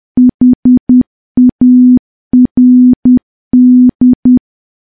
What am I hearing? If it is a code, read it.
Morse code, heard as a keyed tone, about 10 wpm, 250 Hz. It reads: HARD
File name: morse / morse-training-10WPM